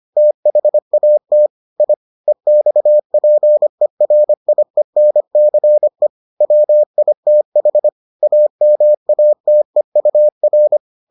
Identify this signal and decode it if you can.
Morse code, heard as a keyed tone, about 25 wpm, 600 Hz. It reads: THAT I EXPERIENCE WITH AMATEUR